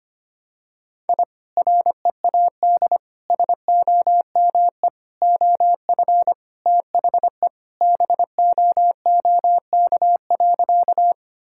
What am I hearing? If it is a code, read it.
Morse code, heard as a keyed tone, about 25 wpm, 700 Hz. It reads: I READ SOME OF THE BOOK.
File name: practice / 25wpm / Sentences with 2-Word Phrases - 25wpm